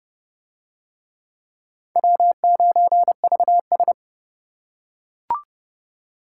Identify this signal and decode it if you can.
Morse code, heard as a keyed tone, about 30 wpm, 700 Hz. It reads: W9VS